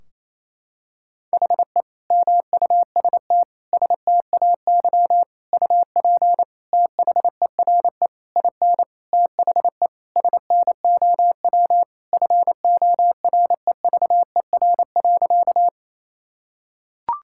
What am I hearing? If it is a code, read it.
Morse code, heard as a keyed tone, about 28 wpm, 700 Hz. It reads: HE MUST STAY UP THERE IN THE SNOW FOREVER.
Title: Sentences from Top 1600 Words 28wpm